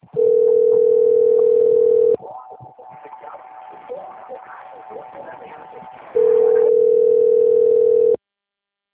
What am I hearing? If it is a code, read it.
Ringback tone (North America), 440 + 480 Hz, with speech.
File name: rg-1600-unknown- - - .10439